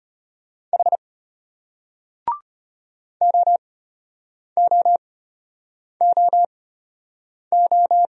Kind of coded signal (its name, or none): Morse code